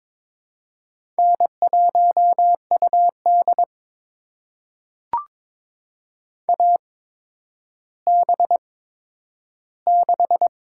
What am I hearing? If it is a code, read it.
Morse code, heard as a keyed tone, about 22 wpm, spaced as for 11 wpm, 700 Hz. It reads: N1UD AB6
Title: USA Call Signs - 8x Character Spacing - 22wpm